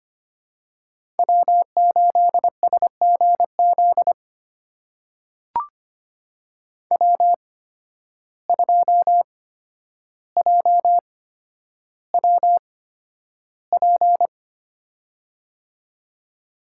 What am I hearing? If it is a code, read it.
Morse code, heard as a keyed tone, about 25 wpm, spaced as for 11 wpm, 700 Hz. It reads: W8SGZ W2JWP
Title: USA Call Signs - 5 Characters - 8x Character Spacing - 25wpm